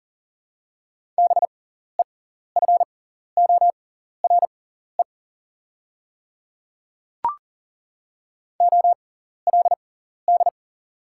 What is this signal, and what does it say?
Morse code, heard as a keyed tone, about 40 wpm, spaced as for 14 wpm, 690 Hz. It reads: BEFORE OLD
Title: 6x Character Spacing - Top 200 Words - 40wpm